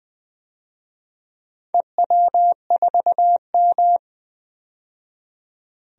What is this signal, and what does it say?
Morse code, heard as a keyed tone, about 20 wpm, 700 Hz. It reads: EW4M
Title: DXCC-100 Call Signs - 4 Characters - 20wpm